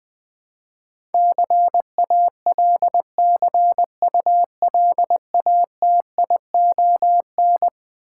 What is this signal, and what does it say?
Morse code, heard as a keyed tone, about 20 wpm, 700 Hz. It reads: CALCULATION
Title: Eleven Letter Words - 20wpm